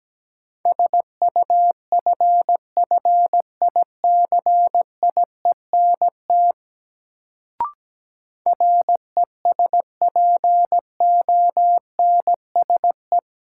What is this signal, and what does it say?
Morse code, heard as a keyed tone, about 17 wpm, 700 Hz. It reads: SUFFICIENT RESPONSE
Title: Sentences from Top 1400 Words - New Word Drill - 17wpm